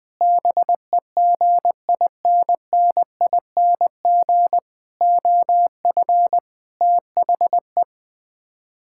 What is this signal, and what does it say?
Morse code, heard as a keyed tone, about 20 wpm, 700 Hz. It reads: BEGINNING OF THE
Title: Build A Phrase - 5-Word Phrases - 20wpm